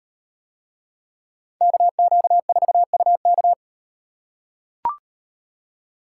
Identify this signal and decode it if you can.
Morse code, heard as a keyed tone, about 37 wpm, 700 Hz. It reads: KQ4UK